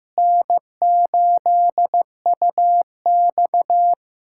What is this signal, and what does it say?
Morse code, heard as a keyed tone, about 15 wpm, 700 Hz. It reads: N8UX
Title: USA Callsign DE Callsign - 4 Characters - 15wpm